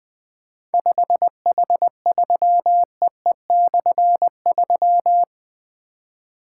Morse code, heard as a keyed tone, about 21 wpm, 700 Hz. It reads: 5H3EE/3
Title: Difficult Call Signs 20wpm